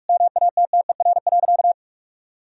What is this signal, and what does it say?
Morse code, heard as a keyed tone, about 46 wpm, 700 Hz. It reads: MATTER.
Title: Sentences from Top 1000 Words - Review Part 3 - 45wpm